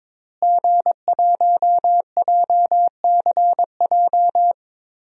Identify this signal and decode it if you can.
Morse code, heard as a keyed tone, about 22 wpm, 700 Hz. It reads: G1JCJ
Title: DXCC-100 Call Signs - 5 Characters - Repeated 3x - 22wpm